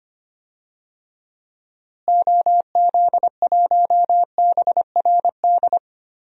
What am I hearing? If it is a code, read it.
Morse code, heard as a keyed tone, about 25 wpm, 700 Hz. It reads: OZ1BRD
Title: DXCC-100 Call Signs - 6 Characters - Repeated 3x - 25wpm